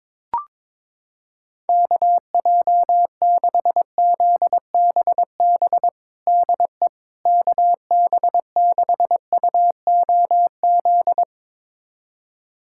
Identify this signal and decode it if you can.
Morse code, heard as a keyed tone, about 22 wpm, 700 Hz. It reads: KJ6ZBB DE KB6UOZ
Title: USA Callsign DE Callsign 22wpm